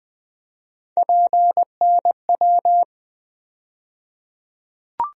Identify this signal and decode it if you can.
Morse code, heard as a keyed tone, about 20 wpm, 700 Hz. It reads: PNW
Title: USA Call Sign Postfixes - 3 Letters 20wpm